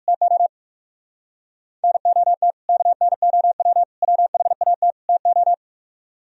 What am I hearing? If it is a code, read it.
Morse code, heard as a keyed tone, about 56 wpm, 700 Hz. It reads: TO NOT KNOW WHAT TO